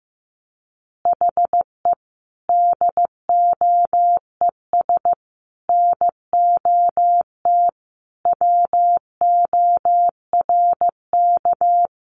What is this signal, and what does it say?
Morse code, heard as a keyed tone, about 15 wpm, 700 Hz. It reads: HE DOES NOT WORK
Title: Additional Easy Phrases 15wpm